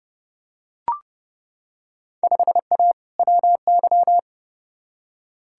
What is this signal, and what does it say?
Morse code, heard as a keyed tone, about 32 wpm, 700 Hz. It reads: 5A WY
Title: Field Day Exchanges Encore - 30wpm